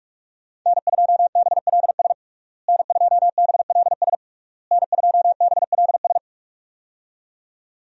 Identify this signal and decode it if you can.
Morse code, heard as a keyed tone, about 42 wpm, 700 Hz. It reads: N2BLS N2BLS N2BLS
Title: USA Call Signs - 5 Characters - Repeated 3x - 45wpm